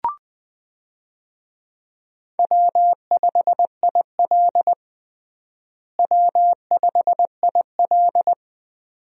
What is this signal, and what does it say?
Morse code, heard as a keyed tone, about 19 wpm, 700 Hz. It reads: W5IL W5IL
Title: USA Call Signs - 4 Characters - Repeated 2x - 20wpm